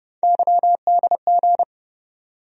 Morse code, heard as a keyed tone, about 30 wpm, 700 Hz. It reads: YDG
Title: USA Call Sign Postfixes - 3 Letters 30wpm